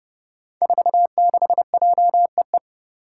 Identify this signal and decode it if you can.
Morse code, heard as a keyed tone, about 31 wpm, 700 Hz. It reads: 46JEE
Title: Five Challenging Letter-Number Combinations 30wpm